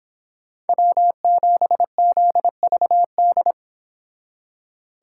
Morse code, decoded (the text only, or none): W7ZVD